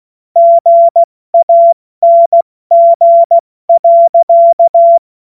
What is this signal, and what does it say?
Morse code, heard as a keyed tone, about 16 wpm, 680 Hz. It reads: GANG.
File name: Morsekurs / Witze / Witz 14. Juni 2022 70 F80